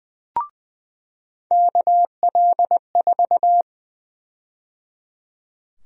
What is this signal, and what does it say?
Morse code, heard as a keyed tone, about 20 wpm, 700 Hz. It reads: KL4
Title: USA Call Sign Prefixes - 20wpm